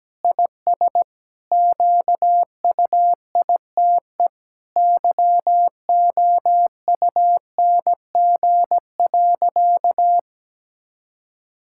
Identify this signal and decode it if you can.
Morse code, heard as a keyed tone, about 17 wpm, 700 Hz. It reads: IS QUITE YOUNG.
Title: Sentences from Top 1000 Words - Review Part 6 - 17wpm